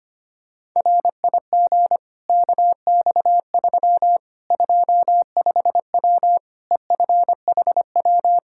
Morse code, heard as a keyed tone, about 25 wpm, 700 Hz. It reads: RIG KX3 25W EFHW
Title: Practice QSO - Exchange 2 - Info - 25wpm